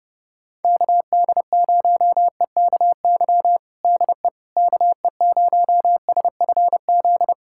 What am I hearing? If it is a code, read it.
Morse code, heard as a keyed tone, about 32 wpm, 700 Hz. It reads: KD0EKY DE KE0SFZ
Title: USA Callsign DE Callsign - 6 Characters - 30wpm